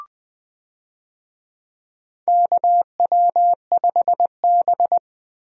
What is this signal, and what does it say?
Morse code, heard as a keyed tone, about 21 wpm, 700 Hz. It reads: KW5B
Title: USA Call Signs - 4 Characters - Repeated 3x - 20wpm